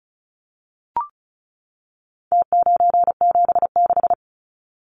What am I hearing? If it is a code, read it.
Morse code, heard as a keyed tone, about 34 wpm, 700 Hz. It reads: T976
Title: Numbers - Contest Style - 4-Digit 35wpm